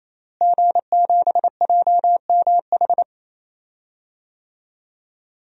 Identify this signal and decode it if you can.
Morse code, heard as a keyed tone, about 28 wpm, 700 Hz. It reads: G7JMH
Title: DXCC-100 Call Signs - 5 Characters - Repeated 3x - 28wpm